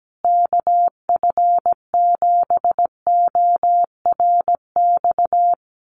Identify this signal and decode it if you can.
Morse code, heard as a keyed tone, about 17 wpm, 700 Hz. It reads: KF7ORX
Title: USA Call Signs Repeated 3x 17wpm